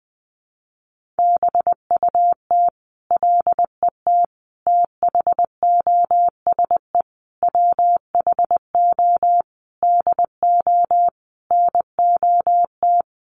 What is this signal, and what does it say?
Morse code, heard as a keyed tone, about 20 wpm, 700 Hz. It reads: BUT LET THOSE WHO DO NOT
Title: Sentences from Top 1500 Words 20wpm